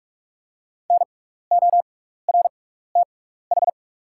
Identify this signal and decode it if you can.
Morse code, heard as a keyed tone, about 44 wpm, spaced as for 15 wpm, 700 Hz. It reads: NORTH
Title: 6x Character Spacing - Top 300 Words - 45wpm